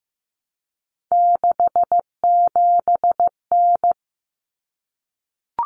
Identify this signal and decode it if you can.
Morse code, heard as a keyed tone, about 15 wpm, 700 Hz. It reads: 67N
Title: Common Cut Numbers - 3-Digit 15wpm